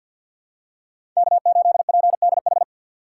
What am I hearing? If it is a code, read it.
Morse code, heard as a keyed tone, about 51 wpm, 700 Hz. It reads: K8PDH
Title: USA Call Signs - 5 Characters - 6x Character Spacing - 50wpm